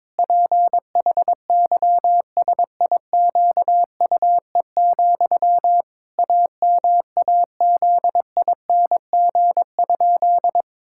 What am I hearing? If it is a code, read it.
Morse code, heard as a keyed tone, about 22 wpm, 700 Hz. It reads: PHYSIQUE, AMAZING?
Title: Pangram Challenge v2 - 22wpm